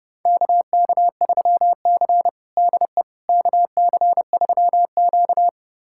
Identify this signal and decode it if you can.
Morse code, heard as a keyed tone, about 32 wpm, 700 Hz. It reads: KK3C DE KC3Q
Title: USA Callsign DE Callsign - 4 Characters - 30wpm